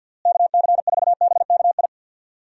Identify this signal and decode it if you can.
Morse code, heard as a keyed tone, about 48 wpm, 700 Hz. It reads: KC4BKI
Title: USA Call Signs - 6 Characters - Repeated 3x - 50wpm